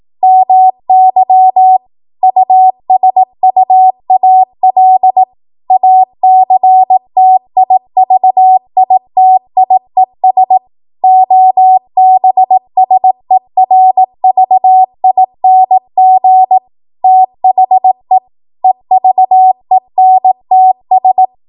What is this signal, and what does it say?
Morse code, heard as a keyed tone, about 18 wpm, 750 Hz. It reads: MY USUAL ACTIVITIES OBSERVING THE EVENTS
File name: Morse / 18WPM